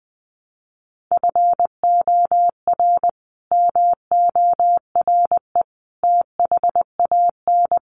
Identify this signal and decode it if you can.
Morse code, heard as a keyed tone, about 20 wpm, 700 Hz. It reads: FOR MORE THAN
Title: Sets of 5 Words 20wpm